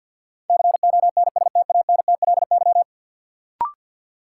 Morse code, heard as a keyed tone, about 53 wpm, 700 Hz. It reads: CONSTANTLY
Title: Sentences from Top 1600 Words - New Word Drill Alt - 50wpm